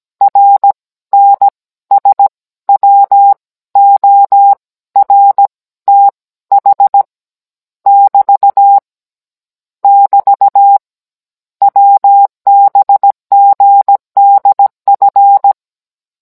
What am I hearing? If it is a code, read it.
Morse code, heard as a keyed tone, about 17 wpm, spaced as for 14 wpm, 800 Hz. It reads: RNSWORTH <BT> <BT> WBGDF
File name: Cours-NR18-16mm